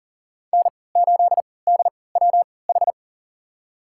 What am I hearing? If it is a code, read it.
Morse code, heard as a keyed tone, about 40 wpm, spaced as for 23 wpm, 700 Hz. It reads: N8DWH
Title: USA Call Signs - 5 Characters - 3x Character Spacing - 40wpm